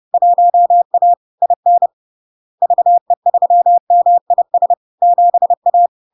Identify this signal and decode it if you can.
Morse code, heard as a keyed tone, about 30 wpm, 700 Hz. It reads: 1A IN VE3MIS 7A
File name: LARK-FD-2022-Log-in-CW-30wpm-700-Hz